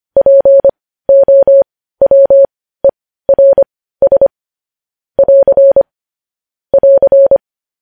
Morse code, heard as a keyed tone, about 25 wpm, spaced as for 15 wpm, 550 Hz. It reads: POWERS <AR> <AR>
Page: News Headlines in Morse Code at 15 WPM